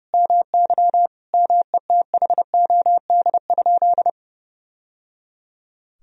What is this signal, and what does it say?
Morse code, heard as a keyed tone, about 30 wpm, 700 Hz. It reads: MY METHOD?
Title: Sentences from Top 1000 Words - Review Part 3 - 30wpm